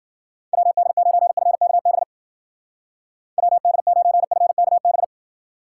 Morse code, heard as a keyed tone, about 52 wpm, 700 Hz. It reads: WD9FKB WD9FKB